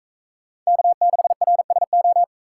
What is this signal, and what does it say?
Morse code, heard as a keyed tone, about 42 wpm, 700 Hz. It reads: K6RSO